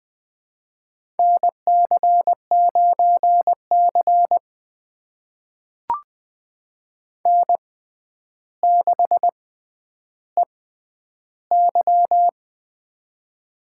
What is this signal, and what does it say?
Morse code, heard as a keyed tone, about 21 wpm, spaced as for 10 wpm, 700 Hz. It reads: NC9C N6EY
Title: USA Call Signs - 6x Character Spacing - 20wpm